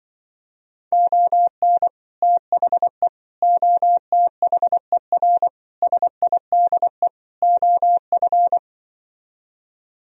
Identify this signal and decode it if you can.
Morse code, heard as a keyed tone, about 23 wpm, 700 Hz. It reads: ON THE OTHER SIDE OF
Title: Speed-Racing - Sets of 5 Words 20wpm